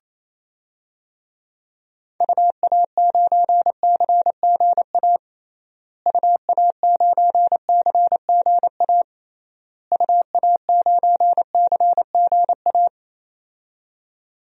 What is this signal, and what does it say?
Morse code, heard as a keyed tone, about 28 wpm, 700 Hz. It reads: UA9CGA UA9CGA UA9CGA
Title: DXCC-100 Call Signs - 6 Characters - Repeated 3x - 28wpm